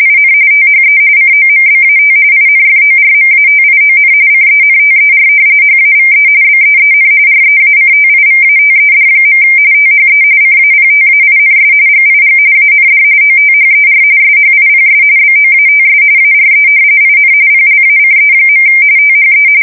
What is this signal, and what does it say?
RTTY, about 45.45 baud. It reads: IN THE COMBAT INFORMATION CENTER HAVE BEEN REACTIVATED. THE CAPTAINS GIG AND OTHER SHIPS BOATS ARE IN PLACE. THE
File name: fsk170 / battleshipnj